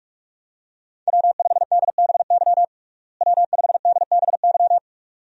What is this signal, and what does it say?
Morse code, heard as a keyed tone, about 42 wpm, 700 Hz. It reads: W5DBY W5DBY
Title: USA Call Signs - 5 Characters - Repeated 2x - 45wpm